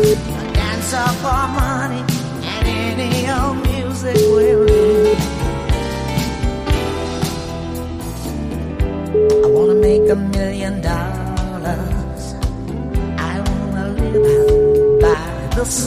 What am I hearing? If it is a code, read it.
Ringback tone (Europe, 425 Hz), with speech.